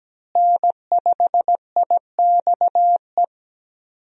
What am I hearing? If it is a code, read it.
Morse code, heard as a keyed tone, about 17 wpm, 700 Hz. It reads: N5IXE
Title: USA Call Signs - 5 Characters - Repeated 3x - 17wpm